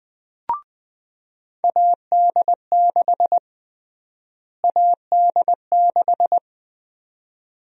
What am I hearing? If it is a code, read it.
Morse code, heard as a keyed tone, about 20 wpm, 700 Hz. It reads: AD6 AD6
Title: USA Call Sign Prefixes - Repeated 2x - 20wpm